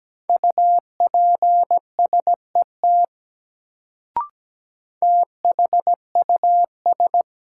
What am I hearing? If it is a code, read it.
Morse code, heard as a keyed tone, about 17 wpm, 700 Hz. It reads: UPSET THUS
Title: Sentences from Top 800 Words - New Word Drill - 17wpm